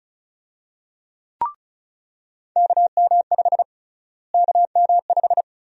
Morse code, heard as a keyed tone, about 34 wpm, 700 Hz. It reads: KM5 KM5
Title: USA Call Sign Prefixes - Repeated 2x - 35wpm